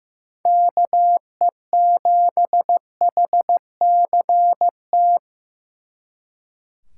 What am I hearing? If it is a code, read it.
Morse code, heard as a keyed tone, about 18 wpm, 700 Hz. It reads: KE7HCT
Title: Speed Racing - USA 2x3 Call Signs 15wpm